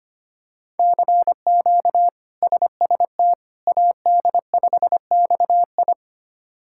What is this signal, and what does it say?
Morse code, heard as a keyed tone, about 25 wpm, 700 Hz. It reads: CQ SST AD5XI
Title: K1USN SST QSO Practice - 25wpm